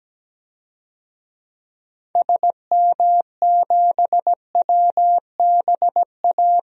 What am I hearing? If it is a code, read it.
Morse code, heard as a keyed tone, about 17 wpm, 700 Hz. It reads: SM7WBA
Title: DXCC-100 Call Signs - 6 Characters - 17wpm